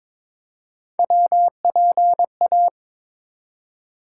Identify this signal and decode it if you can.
Morse code, heard as a keyed tone, about 22 wpm, 700 Hz. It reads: WPA
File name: ARRL Sections v2 - 22wpm